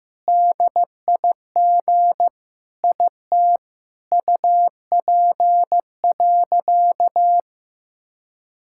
Morse code, heard as a keyed tone, about 15 wpm, 700 Hz. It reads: DIG IT UP.
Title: Sentences from Top 1600 Words 15wpm